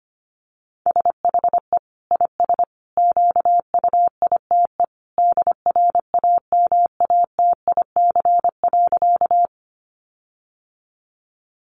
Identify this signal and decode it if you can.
Morse code, heard as a keyed tone, about 25 wpm, 700 Hz. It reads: SHE IS QUITE DRAMATIC.
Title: Sentences from Top 1300 Words 25wpm